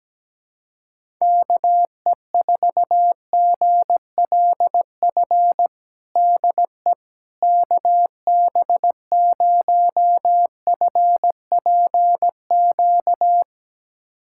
Morse code, heard as a keyed tone, about 17 wpm, 700 Hz. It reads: KE4GLF DE KB0FPQ
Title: USA Callsign DE Callsign 17wpm